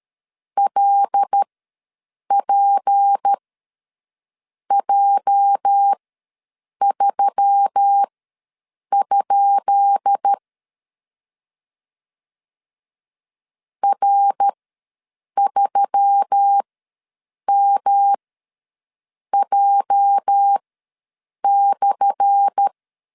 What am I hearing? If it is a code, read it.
Morse code, heard as a keyed tone, about 13 wpm, spaced as for 7 wpm, 800 Hz. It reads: LPJ3? R3MJ/
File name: LEZIONI / 10 SIGNS / 10B INT ws 12wpm 3527